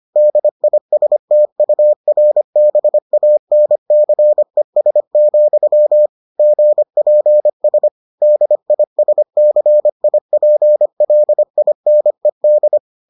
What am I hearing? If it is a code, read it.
Morse code, heard as a keyed tone, about 25 wpm, 600 Hz. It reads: DISTURBANCES, GPS DISCIPLINED